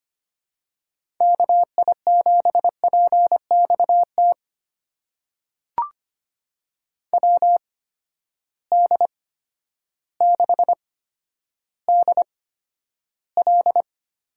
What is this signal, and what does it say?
Morse code, heard as a keyed tone, about 25 wpm, spaced as for 11 wpm, 700 Hz. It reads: KI7PXT WD6DL
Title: USA Call Signs - 6 Characters - 8x Character Spacing - 25wpm